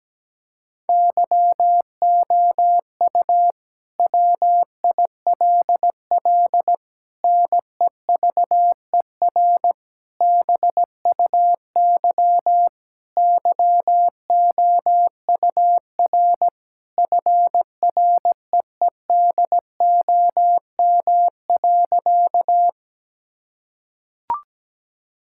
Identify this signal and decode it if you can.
Morse code, heard as a keyed tone, about 17 wpm, 700 Hz. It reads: YOU WILL NEVER BUY YOUR FREEDOM.
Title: Sentences from Top 1000 Words - Review Part 5 - 17wpm